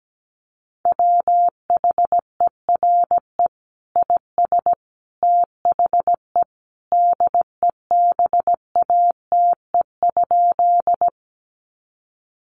Morse code, heard as a keyed tone, about 17 wpm, 700 Hz. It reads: WHERE IS THE DEBATE?
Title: Sentences from Top 1100 Words 17wpm